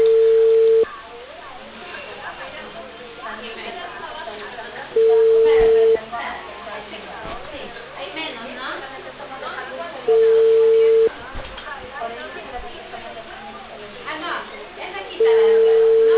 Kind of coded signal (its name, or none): telephone call progress tone